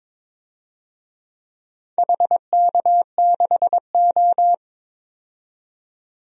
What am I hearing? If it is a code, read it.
Morse code, heard as a keyed tone, about 22 wpm, 700 Hz. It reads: HK6O
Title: DXCC-100 Call Signs - 4 Characters - 22wpm